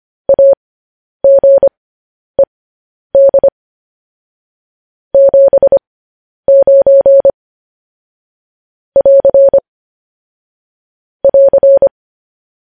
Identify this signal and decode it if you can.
Morse code, heard as a keyed tone, about 25 wpm, spaced as for 10 wpm, 550 Hz. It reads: AGED 79 <AR> <AR>
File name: podcast / News headlines-1x-25x10